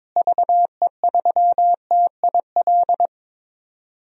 Morse code, heard as a keyed tone, about 22 wpm, 700 Hz. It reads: VE3TIL
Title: DXCC-100 Call Signs - 6 Characters - Repeated 3x - 22wpm